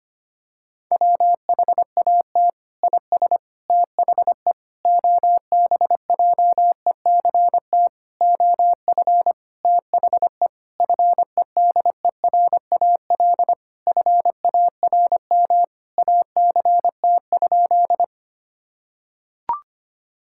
Morse code, heard as a keyed tone, about 25 wpm, 700 Hz. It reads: WHAT IS THE OBJECT OF THE FEDERAL FARM ACT?
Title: Sentences from Top 1000 Words - Review Part 5 - 25wpm